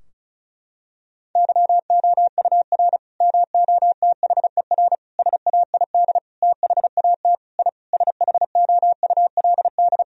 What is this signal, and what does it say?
Morse code, heard as a keyed tone, about 34 wpm, 700 Hz. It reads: YOUR MOTHER SAID THAT I SHOULD
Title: Sentences from Top 1000 Words - Review Part 2 - 35wpm